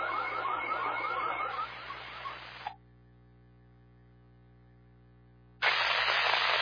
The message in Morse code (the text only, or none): none